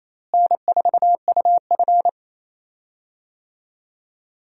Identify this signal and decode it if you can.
Morse code, heard as a keyed tone, about 27 wpm, 700 Hz. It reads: N4UF
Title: USA Call Signs - 4 Characters - Repeated 2x - 28wpm